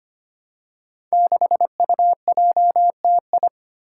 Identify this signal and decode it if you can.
Morse code, heard as a keyed tone, about 25 wpm, 700 Hz. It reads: 6UJTI
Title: Five Challenging Letter-Number Combinations 25wpm